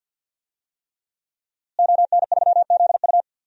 Morse code, heard as a keyed tone, about 51 wpm, 700 Hz. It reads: ON3ZU
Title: DXCC-100 Call Signs - 5 Characters - Repeated 3x - 50wpm